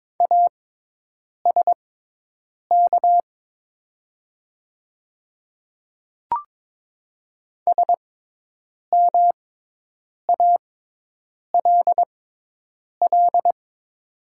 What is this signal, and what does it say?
Morse code, heard as a keyed tone, about 22 wpm, spaced as for 8 wpm, 700 Hz. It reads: ASK SMALL
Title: 6x Character Spacing - Top 200 Words - 22wpm